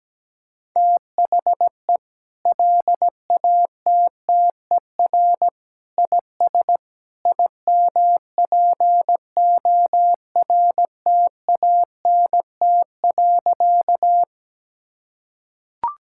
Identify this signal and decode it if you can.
Morse code, heard as a keyed tone, about 17 wpm, 700 Hz. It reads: THE LATTER IS IMPORTANT.
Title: Sentences from Top 1600 Words 17wpm